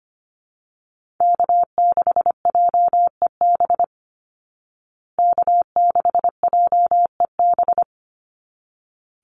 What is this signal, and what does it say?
Morse code, heard as a keyed tone, about 25 wpm, 700 Hz. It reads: K6JEB K6JEB
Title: Call Signs Repeated 3x - 25wpm